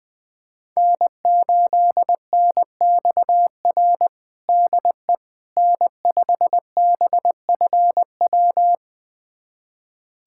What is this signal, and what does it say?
Morse code, heard as a keyed tone, about 20 wpm, 700 Hz. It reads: N8NXR DE N5BFW
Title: USA Callsign DE Callsign - 5 Characters - 20wpm